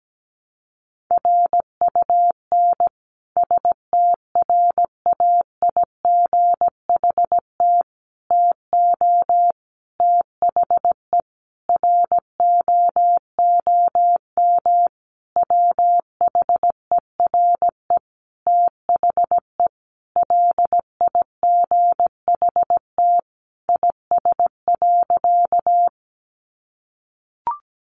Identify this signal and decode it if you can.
Morse code, heard as a keyed tone, about 17 wpm, 700 Hz. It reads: RUN STRAIGHT TO THE ROOM WHERE THE LIGHT IS.
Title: Sentences from Top 1000 Words 17wpm